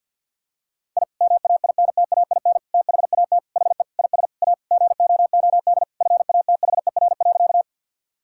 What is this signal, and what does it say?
Morse code, heard as a keyed tone, about 47 wpm, 690 Hz. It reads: I MAINTAIN THAT HE IS A GOOD FATHER.
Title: Sentences from Top 1000 Words - Review Part 5 - 50wpm